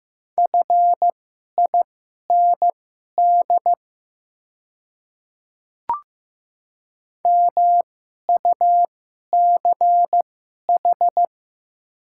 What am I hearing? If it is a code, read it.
Morse code, heard as a keyed tone, about 15 wpm, spaced as for 11 wpm, 700 Hz. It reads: FIND MUCH
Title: 2x Character Spacing - Top 200 Words - 15wpm